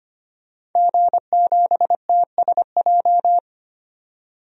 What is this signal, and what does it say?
Morse code, heard as a keyed tone, about 25 wpm, 700 Hz. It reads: G7TSJ
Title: DXCC-100 Call Signs - 5 Characters - 25wpm